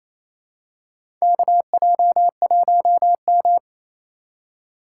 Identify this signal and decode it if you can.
Morse code, heard as a keyed tone, about 28 wpm, 700 Hz. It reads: KJ1M
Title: USA Call Signs - 4 Characters - 2x Character Spacing - 28wpm